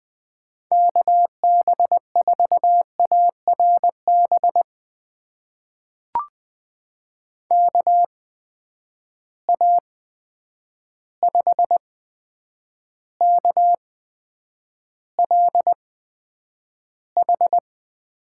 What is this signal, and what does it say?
Morse code, heard as a keyed tone, about 20 wpm, spaced as for 9 wpm, 700 Hz. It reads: KB4ARB KA5KLH
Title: USA Call Signs - 6 Characters - 8x Character Spacing - 20wpm